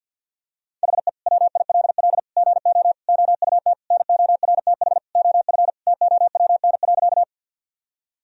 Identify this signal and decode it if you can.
Morse code, heard as a keyed tone, about 53 wpm, 700 Hz. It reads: HE WILL GO OUT NORTH OF TOWN.
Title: Sentences from Top 1000 Words - Review Part 1 - 50wpm